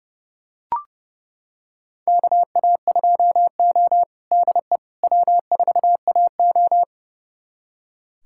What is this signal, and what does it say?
Morse code, heard as a keyed tone, about 30 wpm, 700 Hz. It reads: KA2O DE W4AO
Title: USA Callsign DE Callsign - 4 Characters - 30wpm